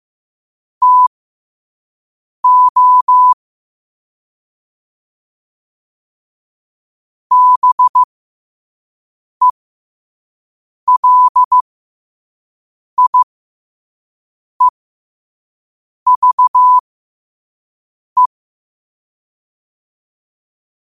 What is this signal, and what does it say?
Morse code, heard as a keyed tone, about 15 wpm, spaced as for 5 wpm, 1000 Hz. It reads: TO BELIEVE
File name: qotdfiles / 5wpm